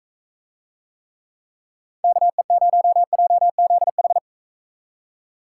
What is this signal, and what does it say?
Morse code, heard as a keyed tone, about 45 wpm, 700 Hz. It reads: KE0JZH